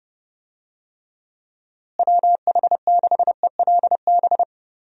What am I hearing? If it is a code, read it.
Morse code, heard as a keyed tone, about 32 wpm, 700 Hz. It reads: WH6ELB
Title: USA Call Signs - 6 Characters - Repeated 3x - 30wpm